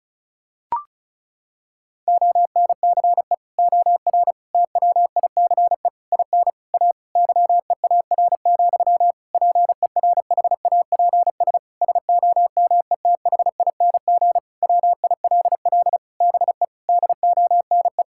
Morse code, heard as a keyed tone, about 36 wpm, 700 Hz. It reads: ONCE OR TWICE IN A YEAR, PERHAPS SOMETHING WILL BE DONE
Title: Sentences from Top 1600 Words 35wpm